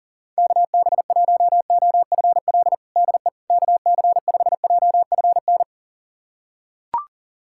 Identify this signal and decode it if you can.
Morse code, heard as a keyed tone, about 40 wpm, 700 Hz. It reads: KB1OFL DE KC5JFN
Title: USA Callsign DE Callsign - 6 Characters - 40wpm